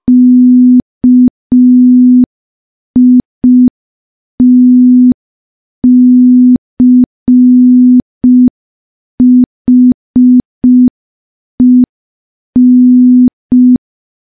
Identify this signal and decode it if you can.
Morse code, heard as a keyed tone, about 5 wpm, 250 Hz. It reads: KITCHEN
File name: morse / morse-training-05WPM